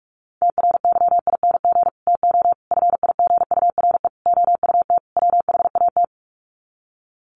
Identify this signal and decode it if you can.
Morse code, heard as a keyed tone, about 44 wpm, 700 Hz. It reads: TRYING TO FIGURE OUT WHAT
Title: Sets of 5 Words 45wpm